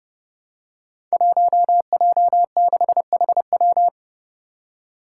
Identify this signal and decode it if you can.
Morse code, heard as a keyed tone, about 30 wpm, 700 Hz. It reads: 1J6HW
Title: Five Challenging Letter-Number Combinations 30wpm